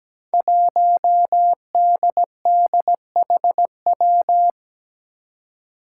Morse code, heard as a keyed tone, about 17 wpm, 700 Hz. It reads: 1DDHW
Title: Five Challenging Letter-Number Combinations 17wpm